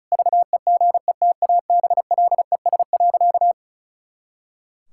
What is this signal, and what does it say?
Morse code, heard as a keyed tone, about 36 wpm, 700 Hz. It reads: VEGETABLES.
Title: Pangram Challenge v2 - 35wpm